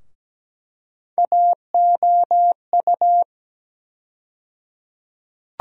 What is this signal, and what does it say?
Morse code, heard as a keyed tone, about 17 wpm, 700 Hz. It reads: AOU
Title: USA Call Sign Postfixes - 3 Letters 17wpm